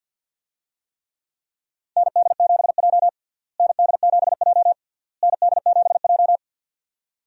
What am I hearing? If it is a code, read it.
Morse code, heard as a keyed tone, about 52 wpm, 700 Hz. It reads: ND7J ND7J ND7J
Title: USA Call Signs - 4 Characters - Repeated 3x - 50wpm